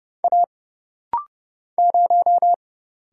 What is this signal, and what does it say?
Morse code, heard as a keyed tone, about 29 wpm, 700 Hz. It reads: A 0
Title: Single Letter-Number - Rapid-Fire 30wpm